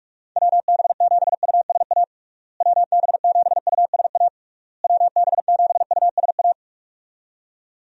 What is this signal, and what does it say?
Morse code, heard as a keyed tone, about 46 wpm, 700 Hz. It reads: WB7USA WB7USA WB7USA
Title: USA Call Signs - 6 Characters - Repeated 3x - 45wpm